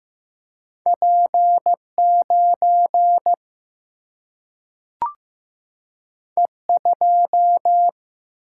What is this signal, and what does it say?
Morse code, heard as a keyed tone, about 15 wpm, 700 Hz. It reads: P9 E2
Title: Single-Letter Single-Number 15wpm